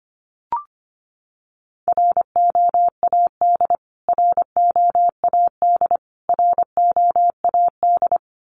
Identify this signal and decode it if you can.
Morse code, heard as a keyed tone, about 25 wpm, 700 Hz. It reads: ROAD ROAD ROAD
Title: Top 500 Words Repeated 25wpm